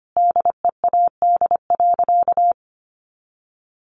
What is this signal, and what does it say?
Morse code, heard as a keyed tone, about 25 wpm, 700 Hz. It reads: DEAD.
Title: Sentences from Top 1500 Words 25wpm